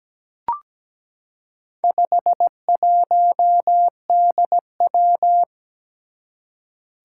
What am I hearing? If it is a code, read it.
Morse code, heard as a keyed tone, about 17 wpm, 700 Hz. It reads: 51DW
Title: Four Challenging Letter-Number Combinations 17wpm